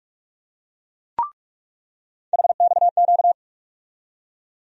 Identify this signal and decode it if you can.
Morse code, heard as a keyed tone, about 46 wpm, 700 Hz. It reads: HXQ
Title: USA Call Sign Postfixes - 3 Letters 45wpm